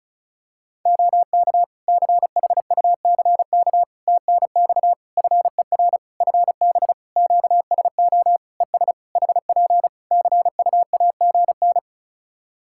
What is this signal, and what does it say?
Morse code, heard as a keyed tone, about 36 wpm, 700 Hz. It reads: OK CHUCK TNX FER FB QSO ES HP CUAGN
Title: Practice QSO - Exchange 4 - Ending - 35wpm